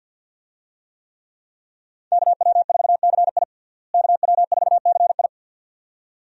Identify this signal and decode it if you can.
Morse code, heard as a keyed tone, about 52 wpm, 700 Hz. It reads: KW4CI KW4CI
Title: USA Call Signs - 5 Characters - Repeated 2x - 50wpm